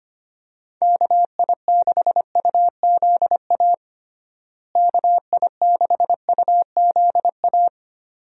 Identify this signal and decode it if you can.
Morse code, heard as a keyed tone, about 25 wpm, 700 Hz. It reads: KI6UZA KI6UZA
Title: USA Call Signs - 6 Characters - Repeated 2x - 25wpm